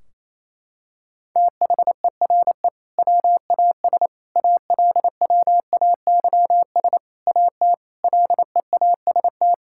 Morse code, heard as a keyed tone, about 28 wpm, 700 Hz. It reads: THERE WAS ALWAYS AT LEAST
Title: Sentences from Top 1600 Words 28wpm